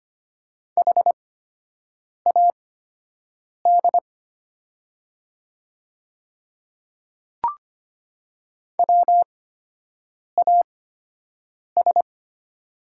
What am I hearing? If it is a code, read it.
Morse code, heard as a keyed tone, about 25 wpm, spaced as for 7 wpm, 700 Hz. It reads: HAD WAS
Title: 8x Character Spacing - Top 100 Words - 25wpm